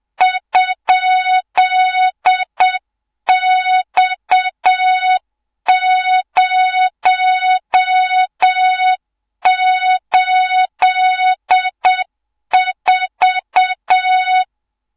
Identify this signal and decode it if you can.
Morse code, heard as a keyed tone, about 7 wpm, 3700 Hz, with noise. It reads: ?X084
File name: Morze-HA2MN / hang / 24 GYAKORLAT